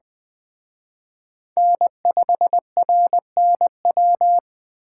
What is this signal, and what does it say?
Morse code, heard as a keyed tone, about 20 wpm, 700 Hz. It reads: N5RNW
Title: Speed Racing - USA Call Signs 17wpm